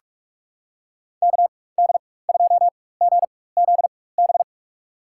Morse code, heard as a keyed tone, about 46 wpm, spaced as for 21 wpm, 700 Hz. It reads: KD2GZB
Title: USA Call Signs - 6 Characters - 4x Character Spacing - 45wpm